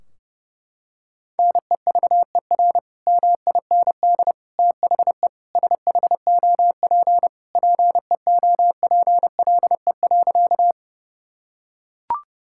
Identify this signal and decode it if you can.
Morse code, heard as a keyed tone, about 32 wpm, 700 Hz. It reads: NEVER MIND THE SHOP PEOPLE.
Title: Sentences from Top 1600 Words 30wpm